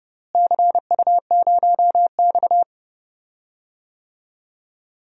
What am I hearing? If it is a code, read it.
Morse code, heard as a keyed tone, about 32 wpm, 700 Hz. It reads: CU0X
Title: DXCC-100 Call Signs - 4 Characters - Repeated 2x - 30wpm